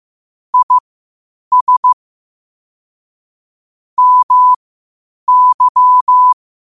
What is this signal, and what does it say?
Morse code, heard as a keyed tone, about 15 wpm, spaced as for 8 wpm, 1000 Hz. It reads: IS MY